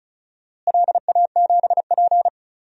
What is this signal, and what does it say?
Morse code, heard as a keyed tone, about 34 wpm, 700 Hz. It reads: LA7P
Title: DXCC-100 Call Signs - 4 Characters - Repeated 2x - 35wpm